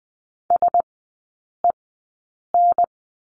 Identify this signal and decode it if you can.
Morse code, heard as a keyed tone, about 20 wpm, spaced as for 8 wpm, 700 Hz. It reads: SEN
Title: 035 Lesson 7e - Letter s - Three Characters 20wpm